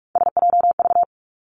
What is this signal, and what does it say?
Morse code, heard as a keyed tone, about 46 wpm, 700 Hz. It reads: SJV
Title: ARRL Sections 45wpm